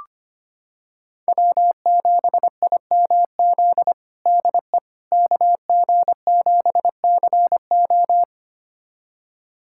Morse code, heard as a keyed tone, about 25 wpm, 700 Hz. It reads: W7IMZ DE KG7CO
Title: USA Callsign DE Callsign - 5 Characters - 25wpm